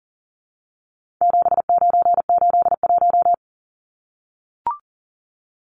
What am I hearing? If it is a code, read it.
Morse code, heard as a keyed tone, about 40 wpm, 700 Hz. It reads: 7981
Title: Numbers - 4-Digit 40wpm